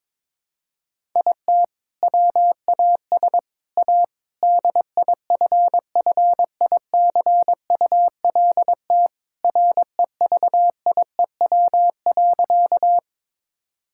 Morse code, heard as a keyed tone, about 22 wpm, 700 Hz. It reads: IT WAS A DIFFICULT REVIEW.
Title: Sentences from Top 1000 Words - Review Part 6 - 22wpm